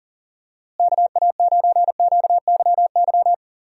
Morse code, heard as a keyed tone, about 39 wpm, 700 Hz. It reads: KA9QYY
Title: USA Call Signs - 6 Characters - Repeated 2x - 40wpm